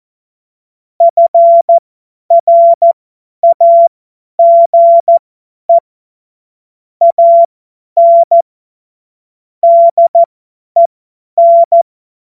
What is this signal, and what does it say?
Morse code, heard as a keyed tone, about 14 wpm, spaced as for 10 wpm, 680 Hz. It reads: FRAGE AN DEN
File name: Morsekurs / Witze / Witz 28.November 2023 50 F70